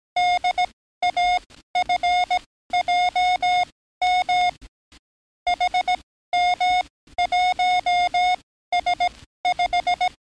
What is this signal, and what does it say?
Morse code, heard as a keyed tone, about 18 wpm, spaced as for 13 wpm, 720 Hz. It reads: DAFJM HM1S5
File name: training / NAVPERS 11049to11060 / NAVPERS 11056 RA RB